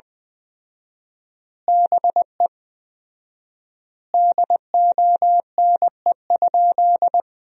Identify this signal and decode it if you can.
Morse code, heard as a keyed tone, about 21 wpm, 700 Hz. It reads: BE DONE?
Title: 4x Word-Spacing - Sentences from Top 500 Words - 20wpm